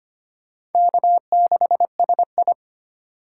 Morse code, heard as a keyed tone, about 30 wpm, 700 Hz. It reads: K6SI
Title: Speed Racing - USA 1x2 Call Signs 25wpm